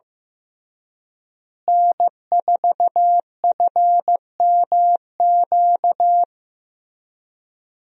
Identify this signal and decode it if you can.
Morse code, heard as a keyed tone, about 15 wpm, 700 Hz. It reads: N4FMQ
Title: USA Call Signs - 5 Characters - Repeated 3x - 15wpm